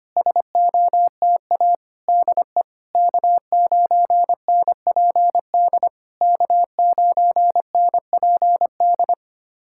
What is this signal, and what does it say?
Morse code, heard as a keyed tone, about 25 wpm, 700 Hz. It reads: SOTA DE K9NPD K9NPD
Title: SOTA QSO Practice 25wpm